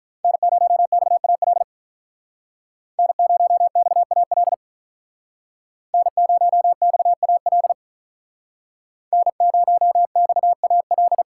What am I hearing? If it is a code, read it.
Morse code, heard as a keyed tone, about 46 wpm, 700 Hz. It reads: N0XAL N0XAL N0XAL N0XAL